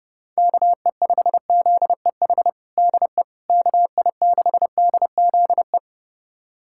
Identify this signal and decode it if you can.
Morse code, heard as a keyed tone, about 30 wpm, 700 Hz. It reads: KE5ZEH DE KI6DZE
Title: USA Callsign DE Callsign 30wpm